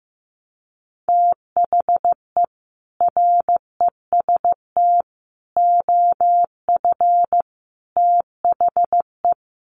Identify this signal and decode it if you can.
Morse code, heard as a keyed tone, about 15 wpm, 700 Hz. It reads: THE REST OF THE
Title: Sets of 5 Words 15wpm